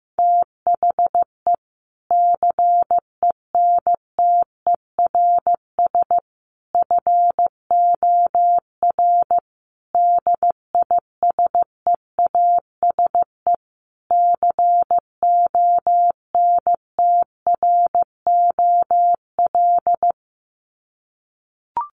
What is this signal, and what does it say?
Morse code, heard as a keyed tone, about 15 wpm, 700 Hz. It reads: THE CENTERS FOR DISEASE CONTROL
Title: Sets of 5 Words Encore2 15wpm